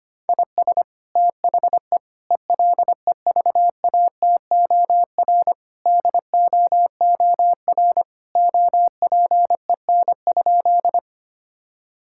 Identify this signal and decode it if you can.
Morse code, heard as a keyed tone, about 25 wpm, 700 Hz. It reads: IS THE ELEVATOR DOOR OPEN?
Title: Sentences from Top 1600 Words 25wpm